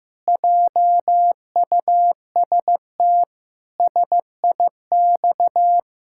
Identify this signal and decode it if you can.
Morse code, heard as a keyed tone, about 15 wpm, 700 Hz. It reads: JUST SIX
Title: Pangram Challenge v2 - 15wpm